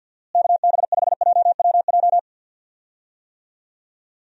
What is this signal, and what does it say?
Morse code, heard as a keyed tone, about 51 wpm, 700 Hz. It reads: KB5JWJ
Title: USA Call Signs - 6 Characters - Repeated 2x - 50wpm